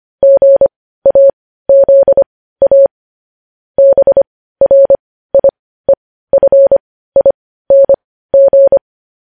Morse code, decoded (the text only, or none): GAZA BRIEFING